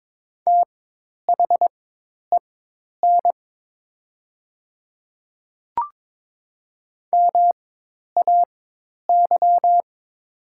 Morse code, heard as a keyed tone, about 22 wpm, spaced as for 10 wpm, 700 Hz. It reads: THEN MAY
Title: 4x Character Spacing - Top 300 Words - 22wpm